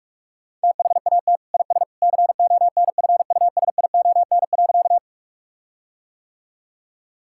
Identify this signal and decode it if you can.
Morse code, heard as a keyed tone, about 46 wpm, 700 Hz. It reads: THAT IS CONFUSION.
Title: Sentences from Top 1600 Words 45wpm